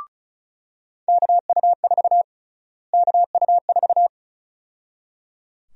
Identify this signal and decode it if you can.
Morse code, heard as a keyed tone, about 34 wpm, 700 Hz. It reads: KU4 KU4
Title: USA Call Sign Prefixes - Repeated 2x - 35wpm